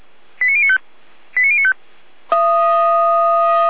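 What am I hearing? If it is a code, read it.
Five-tone selective call (ZVEI), heard as quick runs of five tones, tones of about 70 ms. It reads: 89095 89095
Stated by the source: Five-tone ZVEI selective call received by SDR, archived by Feuerwehr Orlamünde.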